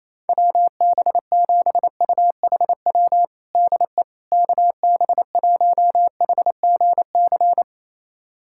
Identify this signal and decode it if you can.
Morse code, heard as a keyed tone, about 27 wpm, 700 Hz. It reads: WB7UHW DE KB1HGC
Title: USA Callsign DE Callsign 28wpm